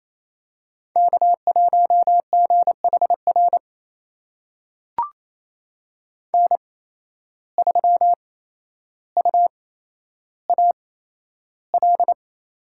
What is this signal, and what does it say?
Morse code, heard as a keyed tone, about 28 wpm, spaced as for 12 wpm, 700 Hz. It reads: K1GHR N3UAL
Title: USA Call Signs - 5 Characters - 8x Character Spacing - 28wpm